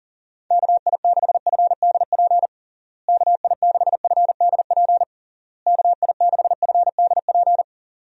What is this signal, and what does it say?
Morse code, heard as a keyed tone, about 37 wpm, 700 Hz. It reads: KI6FDP KI6FDP KI6FDP
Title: USA Call Signs - 6 Characters - Repeated 3x - 40wpm